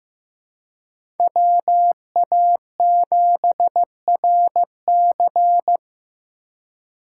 Morse code, heard as a keyed tone, about 15 wpm, 700 Hz. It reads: WA7RC
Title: USA Call Signs - 5 Characters - 4x Character Spacing - 15wpm